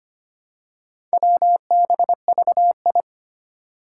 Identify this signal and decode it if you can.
Morse code, heard as a keyed tone, about 25 wpm, 700 Hz. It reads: WBVI
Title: Four Challenging Letter-Number Combinations 25wpm